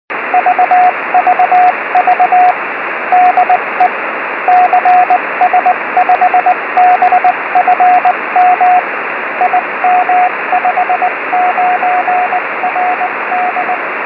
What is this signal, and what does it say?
Morse code, heard as a keyed tone, about 20 wpm, 730 Hz, with radio noise. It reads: VVV DE CS5BFM IM59RD
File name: 70mhz / cs5bfm pe1itr 70166khz